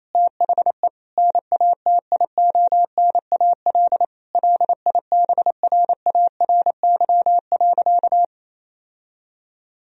Morse code, heard as a keyed tone, about 28 wpm, 700 Hz. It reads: THE NATIONAL LIBRARY.
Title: Sentences from Top 1000 Words - Review Part 4 - 28wpm